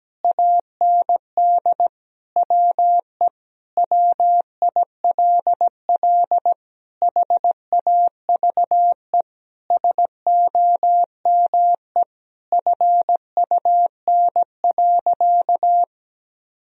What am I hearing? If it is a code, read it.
Morse code, heard as a keyed tone, about 17 wpm, 700 Hz. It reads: AND WE WILL HAVE SOME FUN.
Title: Sentences from Top 1000 Words - Review Part 4 - 17wpm